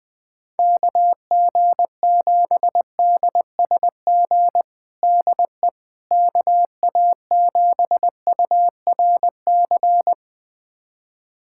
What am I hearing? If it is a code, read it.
Morse code, heard as a keyed tone, about 21 wpm, 700 Hz. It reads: KG7DSG DE KA7URC
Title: USA Callsign DE Callsign - 6 Characters - 20wpm